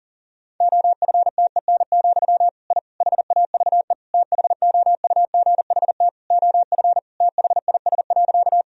Morse code, heard as a keyed tone, about 40 wpm, 700 Hz. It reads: OFTEN, I HAVE THOUGHT OF THIS.
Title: Sentences from Top 1000 Words - Review Part 1 - 40wpm